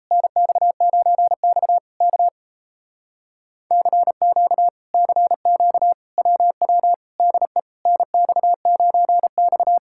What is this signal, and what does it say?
Morse code, heard as a keyed tone, about 38 wpm, 700 Hz. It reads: NX9X K CQ CQ WW DE NX9X